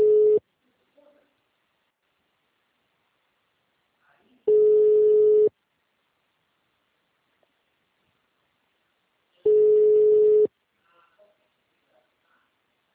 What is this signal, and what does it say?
Ringback tone (Europe, 425 Hz).